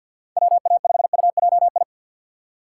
Morse code, heard as a keyed tone, about 53 wpm, 700 Hz. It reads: WA5UJI
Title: USA Call Signs - 6x Character Spacing - 50wpm